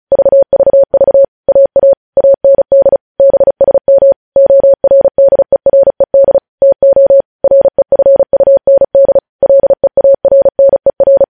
Morse code, heard as a keyed tone, about 36 wpm, 550 Hz. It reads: VVV AA AND BSM ORDERED TO REFUND LEARNER